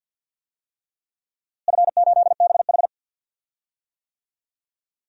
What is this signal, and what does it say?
Morse code, heard as a keyed tone, about 47 wpm, 700 Hz. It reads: F8BH
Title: DXCC-100 Call Signs - 4 Characters - Repeated 3x - 50wpm